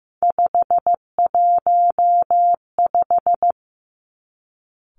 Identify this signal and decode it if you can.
Morse code, heard as a keyed tone, about 15 wpm, 700 Hz. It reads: 515
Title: Three Challenging Letter-Number Combinations 15wpm